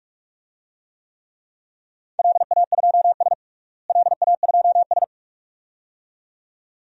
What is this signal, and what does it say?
Morse code, heard as a keyed tone, about 46 wpm, 700 Hz. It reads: LA2S LA2S
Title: DXCC-100 Call Signs - 4 Characters - Repeated 2x - 45wpm